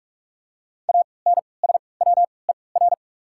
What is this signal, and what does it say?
Morse code, heard as a keyed tone, about 46 wpm, spaced as for 25 wpm, 700 Hz. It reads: ANSWER
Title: 3x Character Spacing - Top 300 Words - 45wpm